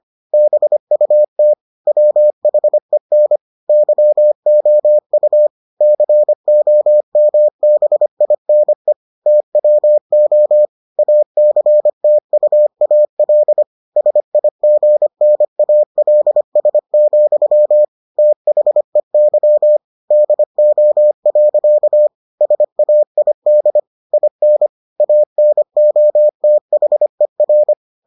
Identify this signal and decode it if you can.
Morse code, heard as a keyed tone, about 25 wpm, 600 Hz. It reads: BUT WHEN YOU COMBINE TWO ACTUAL SIGNALS, THEY DO. SAID IN ANOTHER